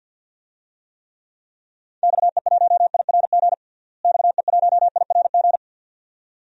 Morse code, heard as a keyed tone, about 47 wpm, 700 Hz. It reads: XE1IRG XE1IRG
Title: DXCC-100 Call Signs - 6 Characters - Repeated 2x - 50wpm